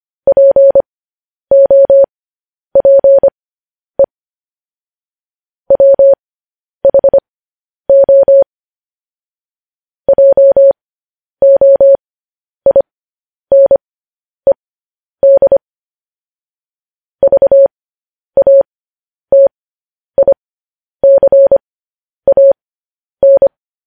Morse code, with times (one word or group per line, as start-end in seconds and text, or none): 0.3-4.0 POPE
5.7-8.4 WHO
10.1-15.6 JOINED
17.2-23.5 VATICAN